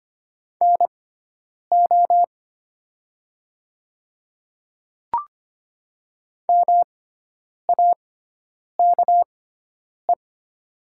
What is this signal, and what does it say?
Morse code, heard as a keyed tone, about 25 wpm, spaced as for 9 wpm, 700 Hz. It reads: NO MAKE
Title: 6x Character Spacing - Top 100 Words - 25wpm